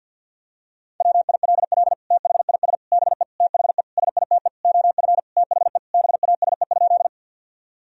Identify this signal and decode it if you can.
Morse code, heard as a keyed tone, about 53 wpm, 700 Hz. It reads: WILL THIS BE THE SITE OF THE BASE?
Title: Sentences from Top 1000 Words - Review Part 1 - 50wpm